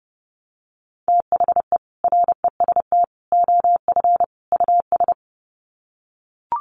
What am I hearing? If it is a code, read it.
Morse code, heard as a keyed tone, about 28 wpm, 700 Hz. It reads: THE REST OF US
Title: Sets of 4 Words 30wpm